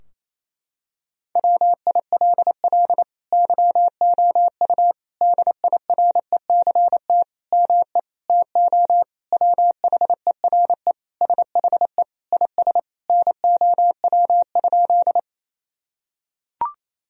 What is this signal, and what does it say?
Morse code, heard as a keyed tone, about 28 wpm, 700 Hz. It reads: WILL YOU DIRECT ME TO WHERE SHE IS NOW?
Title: Sentences from Top 1000 Words - Review Part 2 - 28wpm